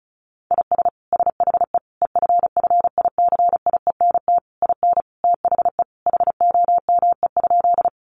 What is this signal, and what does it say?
Morse code, heard as a keyed tone, about 34 wpm, 700 Hz. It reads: IS SHE EFFICIENT IN THE HOME?
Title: Sentences from Top 1300 Words 35wpm